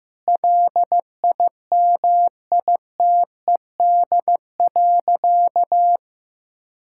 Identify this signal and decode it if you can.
Morse code, heard as a keyed tone, about 16 wpm, 700 Hz. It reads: LIMITED.
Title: Sentences from Top 1000 Words - Review Part 6 - 15wpm